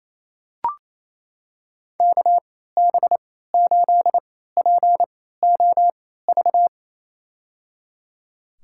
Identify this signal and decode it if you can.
Morse code, heard as a keyed tone, about 27 wpm, spaced as for 16 wpm, 700 Hz. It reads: KB8POV
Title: USA Call Signs - 3x Character Spacing - 28wpm